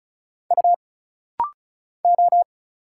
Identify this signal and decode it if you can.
Morse code, heard as a keyed tone, about 36 wpm, 700 Hz. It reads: U O
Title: Single Letters - Rapid-Fire 35wpm